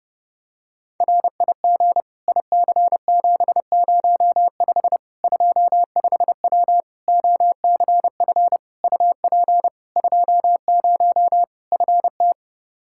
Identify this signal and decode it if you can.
Morse code, heard as a keyed tone, about 29 wpm, 700 Hz. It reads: RIG IC705 25W OCF UP 20 FT
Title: K1USN SST QSO Practice 30wpm